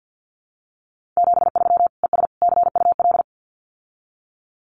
Morse code, heard as a keyed tone, about 48 wpm, 700 Hz. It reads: 73 ES CUL
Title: Common QSO Phrases Encore 50wpm